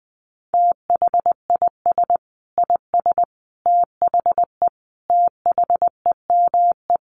Morse code, heard as a keyed tone, about 20 wpm, 700 Hz. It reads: THIS IS THE THEME
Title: Sentences from Top 1100 Words 20wpm